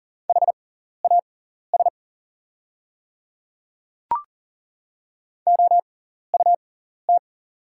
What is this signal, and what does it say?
Morse code, heard as a keyed tone, about 40 wpm, spaced as for 14 wpm, 700 Hz. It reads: HAS OUT
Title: 6x Character Spacing - Top 100 Words - 40wpm